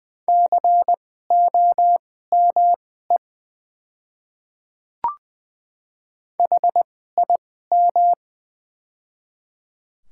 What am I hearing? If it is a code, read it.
Morse code, heard as a keyed tone, about 20 wpm, spaced as for 14 wpm, 700 Hz. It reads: COME HIM
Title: 2x Character Spacing - Top 100 Words - 20wpm